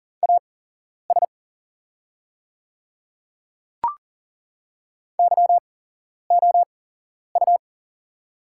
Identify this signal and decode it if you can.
Morse code, heard as a keyed tone, about 40 wpm, spaced as for 11 wpm, 700 Hz. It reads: AS YOU
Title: 8x Character Spacing - Top 200 Words - 40wpm